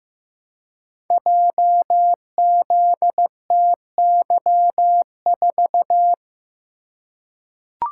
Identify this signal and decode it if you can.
Morse code, heard as a keyed tone, about 15 wpm, 700 Hz. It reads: JZTY4
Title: Code Groups - 5 Letter:Numbers - 15wpm